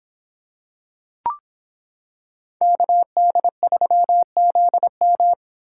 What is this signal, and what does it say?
Morse code, heard as a keyed tone, about 25 wpm, 700 Hz. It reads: KD3ZM